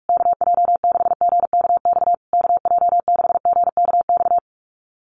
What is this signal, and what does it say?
Morse code, heard as a keyed tone, about 44 wpm, 700 Hz. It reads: KJ6GKX KJ6GKX
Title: USA Call Signs Repeated 3x 45wpm